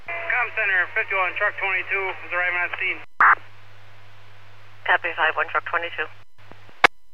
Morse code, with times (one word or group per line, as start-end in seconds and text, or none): none